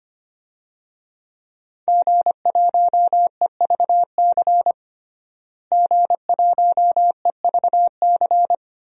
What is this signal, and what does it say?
Morse code, heard as a keyed tone, about 25 wpm, 700 Hz. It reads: G1EVC G1EVC
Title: DXCC-100 Call Signs - 5 Characters - Repeated 3x - 25wpm